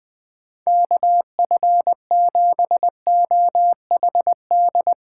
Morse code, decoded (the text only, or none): KF7OHD